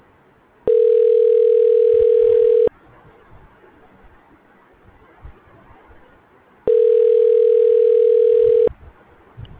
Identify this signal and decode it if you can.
Ringback tone (North America), 440 + 480 Hz.
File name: recordings / REe bc44e2999f78f270b74e460c